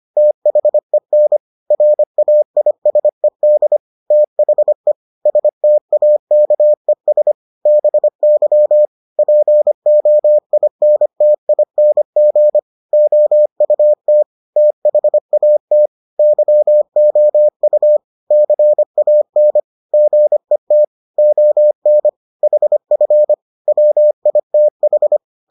Morse code, heard as a keyed tone, about 25 wpm, 600 Hz. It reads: THEN RAISED THE STAKES BY POINTING OUT THAT YOU CAN GET ON HF WITH